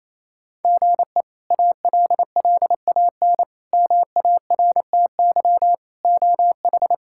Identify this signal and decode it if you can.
Morse code, heard as a keyed tone, about 28 wpm, 700 Hz. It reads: GE ALLAN MARTY OH
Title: K1USN SST QSO Practice - 28wpm